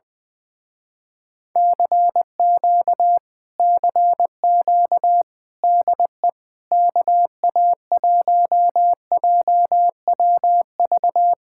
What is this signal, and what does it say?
Morse code, heard as a keyed tone, about 20 wpm, 700 Hz. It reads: CQ CQ DE KA1JWV
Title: Speed-Racing Common QSO Phrases - 20wpm